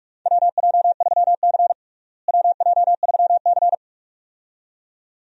Morse code, heard as a keyed tone, about 41 wpm, 700 Hz. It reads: WJ3C WJ3C
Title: USA Call Signs - 4 Characters - Repeated 2x - 45wpm